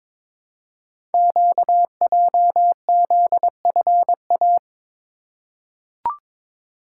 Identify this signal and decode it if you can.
Morse code, heard as a keyed tone, about 22 wpm, 700 Hz. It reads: QJZFA
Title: Code Groups - 5 Letters - 22wpm